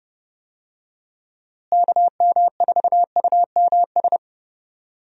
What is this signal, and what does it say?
Morse code, heard as a keyed tone, about 32 wpm, 700 Hz. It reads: KM4UMS
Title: USA Call Signs - 6 Characters - Repeated 2x - 30wpm